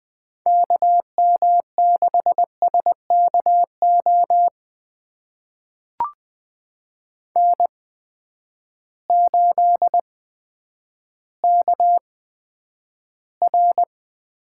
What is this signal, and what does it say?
Morse code, heard as a keyed tone, about 20 wpm, spaced as for 10 wpm, 700 Hz. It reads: KM6SKO N8KR
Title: USA Call Signs - 8x Character Spacing - 20wpm